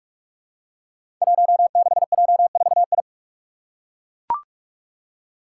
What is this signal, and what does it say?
Morse code, heard as a keyed tone, about 41 wpm, 700 Hz. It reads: 16J4I
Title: Five Challenging Letter-Number Combinations 45wpm